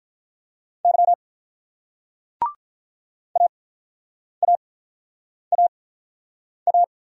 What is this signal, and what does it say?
Morse code, heard as a keyed tone, about 46 wpm, spaced as for 9 wpm, 700 Hz. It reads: Y AAAA